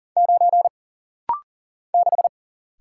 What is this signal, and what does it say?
Morse code, heard as a keyed tone, about 40 wpm, 700 Hz. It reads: 9 6
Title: Single Letter-Number - Rapid-Fire 40wpm